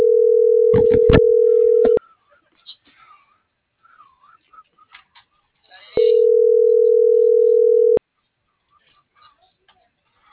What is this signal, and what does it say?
Ringback tone (North America), 440 + 480 Hz.
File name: CENTRAL MAYAB / out raulcristinomayab - .6221